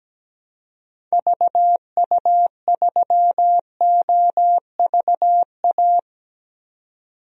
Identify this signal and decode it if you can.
Morse code, heard as a keyed tone, about 17 wpm, 700 Hz. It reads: VU3OVA
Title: DXCC-100 Call Signs - 6 Characters - 17wpm